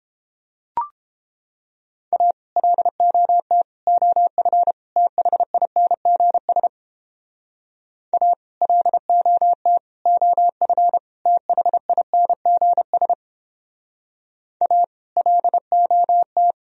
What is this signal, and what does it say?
Morse code, heard as a keyed tone, about 31 wpm, 700 Hz. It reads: A LOT OF THINGS A LOT OF THINGS A LOT